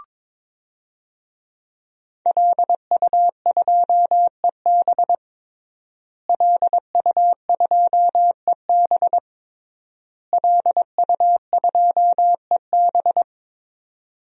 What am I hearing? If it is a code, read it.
Morse code, heard as a keyed tone, about 22 wpm, 700 Hz. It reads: LU2EB LU2EB LU2EB
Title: DXCC-100 Call Signs - 5 Characters - Repeated 3x - 22wpm